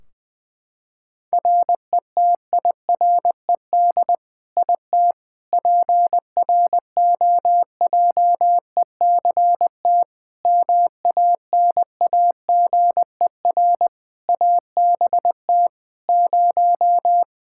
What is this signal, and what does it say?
Morse code, heard as a keyed tone, about 20 wpm, 700 Hz. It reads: RETIRED IT PROJECT MANAGER ABT 0
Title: Practice QSO - Exchange 3 - More Info - 20wpm